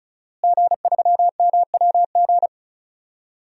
Morse code, heard as a keyed tone, about 36 wpm, 700 Hz. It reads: G3MWG
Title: DXCC-100 Call Signs - 5 Characters - Repeated 3x - 35wpm